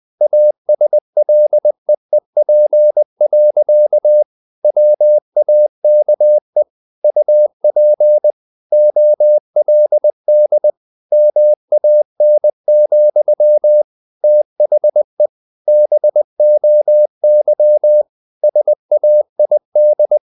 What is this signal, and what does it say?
Morse code, heard as a keyed tone, about 20 wpm, 600 Hz. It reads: ASLEEP. WAKE UP OLD MAN, THE BOY SAID
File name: the old man and the sea / the old man and the sea part 1 20wpm